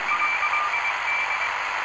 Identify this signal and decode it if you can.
Morse code, heard as a keyed tone, about 21 wpm, 2350 Hz, with noise. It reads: LID
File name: attachments / 20.11.2017 PPP s Converterom 10mHz